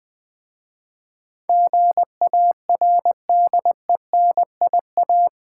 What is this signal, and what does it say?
Morse code, heard as a keyed tone, about 20 wpm, 700 Hz. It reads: GARDENIA
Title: Difficult Words with Hints - 20wpm